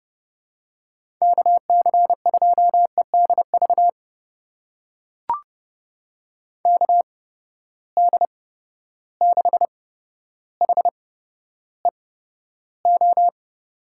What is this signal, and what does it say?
Morse code, heard as a keyed tone, about 30 wpm, spaced as for 13 wpm, 700 Hz. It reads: KC2EDV KD6HEO
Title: USA Call Signs - 6 Characters - 8x Character Spacing - 30wpm